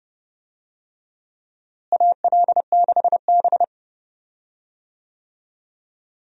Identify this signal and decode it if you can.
Morse code, heard as a keyed tone, about 30 wpm, 700 Hz. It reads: AL6B
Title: USA Call Signs - 4 Characters - Repeated 3x - 30wpm